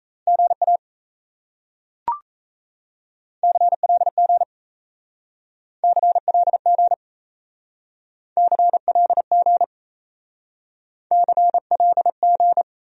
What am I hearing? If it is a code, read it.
Morse code, heard as a keyed tone, about 37 wpm, 700 Hz. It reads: GA CLG CLG CLG CLG